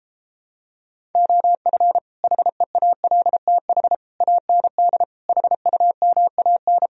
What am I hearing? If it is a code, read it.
Morse code, heard as a keyed tone, about 29 wpm, 700 Hz. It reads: OF HEALTH AND HUMAN